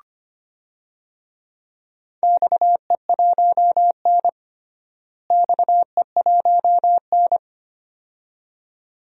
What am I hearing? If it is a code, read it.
Morse code, heard as a keyed tone, about 25 wpm, 700 Hz. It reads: XE1N XE1N
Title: DXCC-100 Call Signs - 4 Characters - Repeated 2x - 25wpm